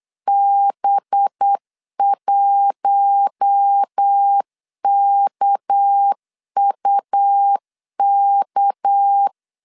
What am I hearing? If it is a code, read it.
Morse code, heard as a keyed tone, about 8 wpm, 800 Hz. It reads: B1KUK
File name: LEZIONI / 6 PKX0 / 6 PK 8wpm 3003